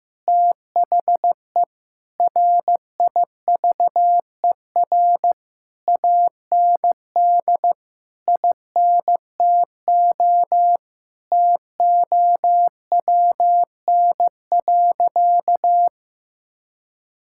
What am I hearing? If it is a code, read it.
Morse code, heard as a keyed tone, about 16 wpm, 700 Hz. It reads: THE RIVER AND INTO TOWN.
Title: Sentences from Top 1000 Words - Review Part 6 - 15wpm